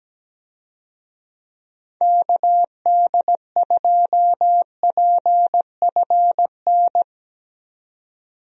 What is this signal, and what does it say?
Morse code, heard as a keyed tone, about 17 wpm, 700 Hz. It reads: KD2PFN
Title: USA Call Signs - 6 Characters - 17wpm